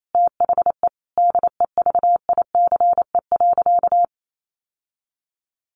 Morse code, heard as a keyed tone, about 28 wpm, 700 Hz. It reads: THE DEVICE.
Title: Sentences from Top 1500 Words 28wpm